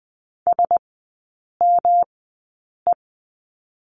Morse code, heard as a keyed tone, about 20 wpm, spaced as for 8 wpm, 700 Hz. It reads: SME
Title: 107 Lesson 18e - Letter m - Three Characters 20wpm